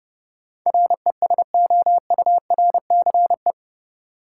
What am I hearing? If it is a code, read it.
Morse code, heard as a keyed tone, about 31 wpm, 700 Hz. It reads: RESOURCE
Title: Sentences from Top 1300 Words - New Word Drill Alt - 30wpm